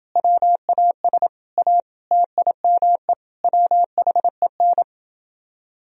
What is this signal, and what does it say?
Morse code, heard as a keyed tone, about 30 wpm, 700 Hz. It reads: WAS A TIME WHEN